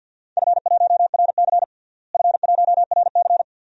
Morse code, heard as a keyed tone, about 50 wpm, 700 Hz. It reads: F1RC F1RC
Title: DXCC-100 Call Signs - 4 Characters - Repeated 3x - 50wpm